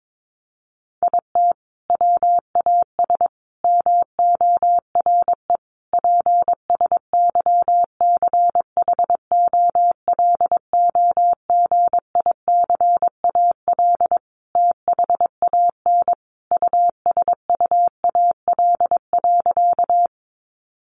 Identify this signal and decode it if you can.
Morse code, heard as a keyed tone, about 22 wpm, 700 Hz. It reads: IT WAS MORE PSYCHOLOGICAL THAN USUAL.
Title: Sentences from Top 1400 Words 22wpm